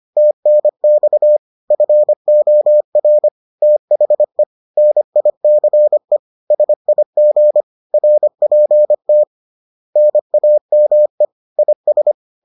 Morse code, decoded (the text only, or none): TNX FOR THE NICE SIG RPT NAME IS